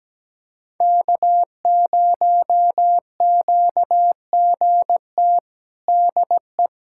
Morse code, heard as a keyed tone, about 17 wpm, 700 Hz. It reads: K0QGT DE
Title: USA Callsign DE Callsign - 5 Characters - 17wpm